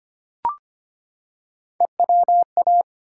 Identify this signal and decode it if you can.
Morse code, heard as a keyed tone, about 25 wpm, 700 Hz. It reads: EWA
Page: morse-code-ninja-practice-25wpm